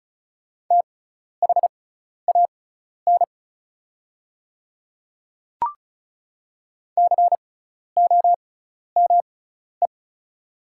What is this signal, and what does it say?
Morse code, heard as a keyed tone, about 36 wpm, spaced as for 12 wpm, 700 Hz. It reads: THAN COME
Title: 6x Character Spacing - Top 100 Words - 35wpm